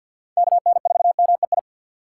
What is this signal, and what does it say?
Morse code, heard as a keyed tone, about 52 wpm, 700 Hz. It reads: K N 4 M E I
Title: USA Call Signs - 6 Characters - 8x Character Spacing - 50wpm